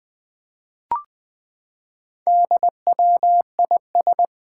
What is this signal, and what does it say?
Morse code, heard as a keyed tone, about 21 wpm, 700 Hz. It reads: DWIS
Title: Four Challenging Letter-Number Combinations 20wpm